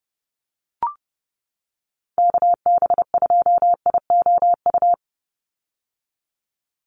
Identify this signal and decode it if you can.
Morse code, heard as a keyed tone, about 30 wpm, 700 Hz. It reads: KB2IOU
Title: USA Call Signs 30wpm